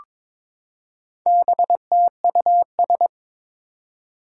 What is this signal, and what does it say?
Morse code, heard as a keyed tone, about 22 wpm, 700 Hz. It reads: BTUS
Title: Four Challenging Letter-Number Combinations 22wpm